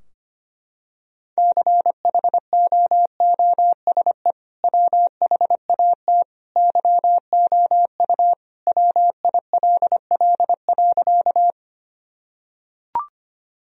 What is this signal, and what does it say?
Morse code, heard as a keyed tone, about 25 wpm, 700 Hz. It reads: CHOOSE WHAT YOU WILL.
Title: Sentences from Top 1000 Words - Review Part 4 - 25wpm